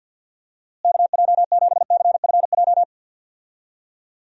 Morse code, heard as a keyed tone, about 48 wpm, 700 Hz. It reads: KJ7CFJ
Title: USA Call Signs - 6 Characters - Repeated 2x - 50wpm